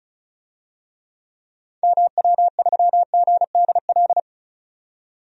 Morse code, heard as a keyed tone, about 36 wpm, 700 Hz. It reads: MW3GDL
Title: DXCC-100 Call Signs - 6 Characters - 35wpm